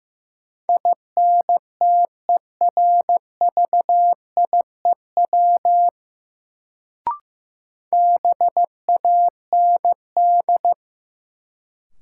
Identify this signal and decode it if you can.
Morse code, heard as a keyed tone, about 16 wpm, 700 Hz. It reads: INTERVIEW BAND
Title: Sentences from Top 1600 Words - New Word Drill - 15wpm